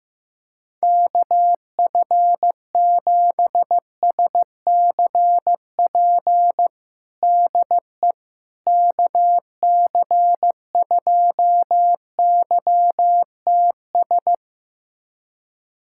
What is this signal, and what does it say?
Morse code, heard as a keyed tone, about 15 wpm, 700 Hz. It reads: KF7SCP DE KC2YTS
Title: USA Callsign DE Callsign - 6 Characters - 15wpm